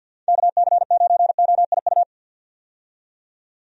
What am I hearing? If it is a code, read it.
Morse code, heard as a keyed tone, about 51 wpm, 700 Hz. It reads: KC9OIU
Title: USA Call Signs - 6 Characters - 50wpm